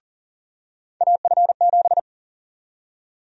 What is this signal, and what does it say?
Morse code, heard as a keyed tone, about 40 wpm, 700 Hz. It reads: AF7
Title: USA Call Sign Prefixes - 40wpm